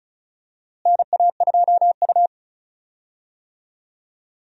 Morse code, heard as a keyed tone, about 36 wpm, 700 Hz. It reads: NA2U
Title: USA Call Signs - 4 Characters - Repeated 2x - 35wpm